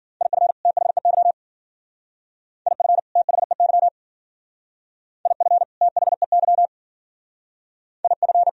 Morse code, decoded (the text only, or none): IF THEY IF THEY IF THEY IF